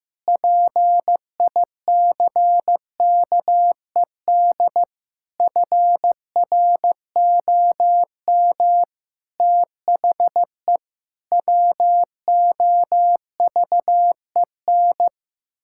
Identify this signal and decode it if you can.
Morse code, heard as a keyed tone, about 15 wpm, 700 Hz. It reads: PICKED FROM THE WOVEN
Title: Pangram Challenge v2 - 15wpm